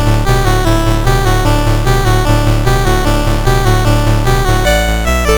There is no code.